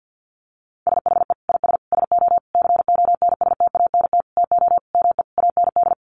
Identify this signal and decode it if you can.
Morse code, heard as a keyed tone, about 50 wpm, 700 Hz. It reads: SHE IS SO CONSTANT TO ME AND
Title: Sentences from Top 1500 Words 50wpm